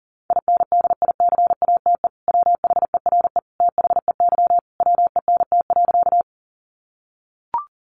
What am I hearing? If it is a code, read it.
Morse code, heard as a keyed tone, about 40 wpm, 700 Hz. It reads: INDICATE WHERE THEY WENT.
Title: Sentences from Top 1300 Words 40wpm